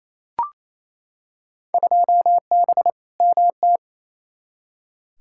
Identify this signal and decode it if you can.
Morse code, heard as a keyed tone, about 28 wpm, 700 Hz. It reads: 2B MT
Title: Field Day Exchanges Encore - 28wpm